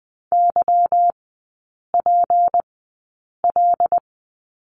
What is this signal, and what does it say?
Morse code, heard as a keyed tone, about 20 wpm, spaced as for 8 wpm, 690 Hz. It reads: YPL
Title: 191 Lesson 31e - Letter v - Three Characters 20wpm